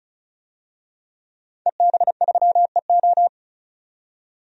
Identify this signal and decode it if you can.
Morse code, heard as a keyed tone, about 36 wpm, 700 Hz. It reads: EB3EO
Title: DXCC-100 Call Signs - 5 Characters - 35wpm